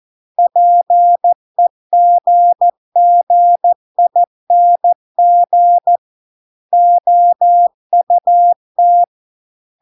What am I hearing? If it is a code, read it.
Morse code, heard as a keyed tone, about 14 wpm, 700 Hz. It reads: PEGGING OUT